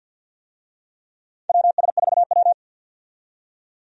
Morse code, heard as a keyed tone, about 53 wpm, 700 Hz. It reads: WS4W
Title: USA Call Signs - 4 Characters - 50wpm